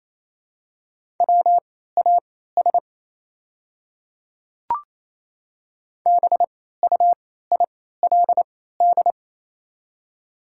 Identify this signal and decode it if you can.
Morse code, heard as a keyed tone, about 28 wpm, spaced as for 16 wpm, 700 Hz. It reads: WAS BUILD
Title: 3x Character Spacing - Top 300 Words - 28wpm